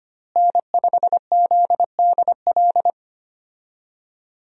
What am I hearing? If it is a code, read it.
Morse code, heard as a keyed tone, about 25 wpm, 700 Hz. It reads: N5ZDL
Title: USA Call Signs - 5 Characters - Repeated 3x - 25wpm